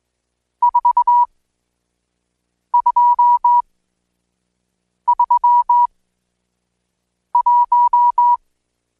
Morse code, heard as a keyed tone, about 20 wpm, spaced as for 5 wpm, 970 Hz. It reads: 4231
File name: Code Quick / AudioDownld1 / Seg76-Day8-DayDrill 8